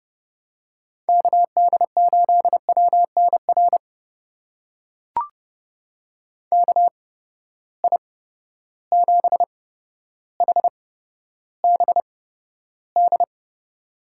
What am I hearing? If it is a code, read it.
Morse code, heard as a keyed tone, about 30 wpm, spaced as for 13 wpm, 700 Hz. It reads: KD8WNR KI7HBD
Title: USA Call Signs - 6 Characters - 8x Character Spacing - 30wpm